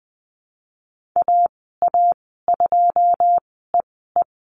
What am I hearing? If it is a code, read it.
Morse code, heard as a keyed tone, about 20 wpm, spaced as for 15 wpm, 700 Hz. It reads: AA2EE
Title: 122 Lesson 20g - Number 3 - Call Signs 20wpm